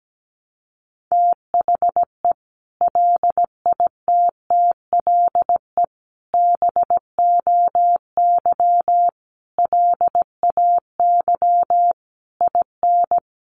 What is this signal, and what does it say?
Morse code, heard as a keyed tone, about 17 wpm, 700 Hz. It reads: THE LITTLE BOY LAY IN
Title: Sentences from Top 500 Words 17wpm